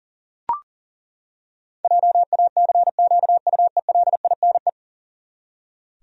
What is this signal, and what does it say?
Morse code, heard as a keyed tone, about 40 wpm, 700 Hz. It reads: JACQUELINE
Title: USA Common Names - 40wpm